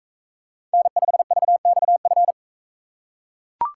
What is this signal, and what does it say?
Morse code, heard as a keyed tone, about 39 wpm, 700 Hz. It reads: N5VXF